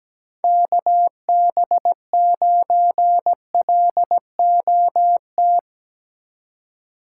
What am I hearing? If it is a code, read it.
Morse code, heard as a keyed tone, about 18 wpm, 700 Hz. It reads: KB9LOT
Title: Speed Racing - USA Call Signs 17wpm